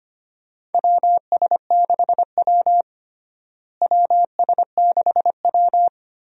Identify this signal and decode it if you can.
Morse code, heard as a keyed tone, about 25 wpm, 700 Hz. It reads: WS6W WS6W
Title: USA Call Signs - 4 Characters - Repeated 3x - 25wpm